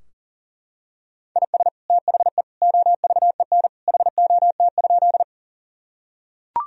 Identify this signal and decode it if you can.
Morse code, heard as a keyed tone, about 40 wpm, 700 Hz. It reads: IS THE OVEN HOT?
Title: Sentences from Top 1000 Words - Review Part 4 - 40wpm